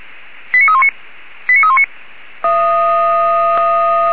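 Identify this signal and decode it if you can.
Five-tone selective call (ZVEI), heard as quick runs of five tones, tones of about 70 ms. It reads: 89319 89319
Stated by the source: ZVEI five-tone selective call received by SDR.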